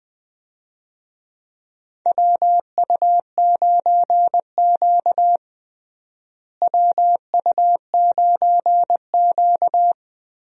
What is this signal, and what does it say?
Morse code, heard as a keyed tone, about 20 wpm, 700 Hz. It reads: WU9Q WU9Q
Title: USA Call Signs - 4 Characters - Repeated 2x - 20wpm